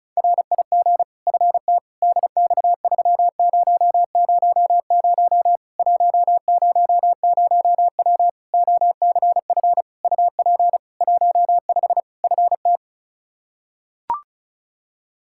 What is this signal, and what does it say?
Morse code, heard as a keyed tone, about 36 wpm, 700 Hz. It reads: RIG FT DX3000 100W OCF UP 15 FT
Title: K1USN SST QSO Practice 35wpm